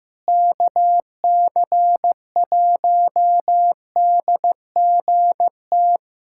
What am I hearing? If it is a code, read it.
Morse code, heard as a keyed tone, about 14 wpm, 700 Hz. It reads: KC1DGT
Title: USA Callsign DE Callsign - 6 Characters - 15wpm